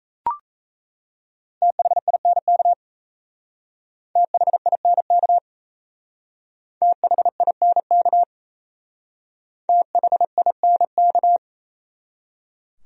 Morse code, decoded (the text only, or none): THINK THINK THINK THINK